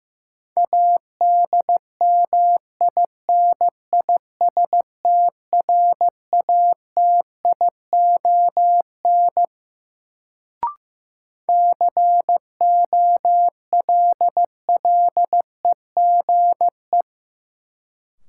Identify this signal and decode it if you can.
Morse code, heard as a keyed tone, about 15 wpm, 700 Hz. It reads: ADMINISTRATION COLLEGE
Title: Sentences from Top 1300 Words - New Word Drill - 15wpm